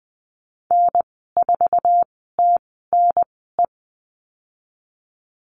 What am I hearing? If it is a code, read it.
Morse code, heard as a keyed tone, about 20 wpm, spaced as for 15 wpm, 700 Hz. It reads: N4TNE
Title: 173 Lesson 28g - Number 9 - Call Signs 20wpm